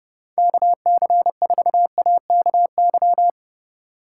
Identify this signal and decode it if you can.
Morse code, heard as a keyed tone, about 31 wpm, 700 Hz. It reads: KC4AKY
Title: Speed Racing - USA 2x3 Call Signs 25wpm